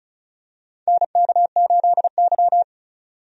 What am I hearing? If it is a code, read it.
Morse code, heard as a keyed tone, about 34 wpm, 700 Hz. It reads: NK8Y
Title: USA Call Signs - 4 Characters - 35wpm